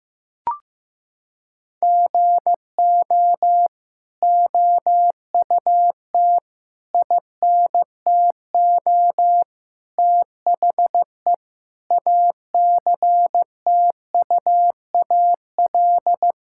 Morse code, heard as a keyed tone, about 15 wpm, 700 Hz. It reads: GO OUT INTO THE ACTUAL
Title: Sentences from Top 1000 Words - Review Part 5 - 15wpm